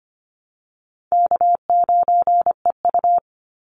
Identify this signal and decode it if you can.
Morse code, heard as a keyed tone, about 25 wpm, 700 Hz. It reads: K9EU
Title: Call Signs Repeated 3x - 25wpm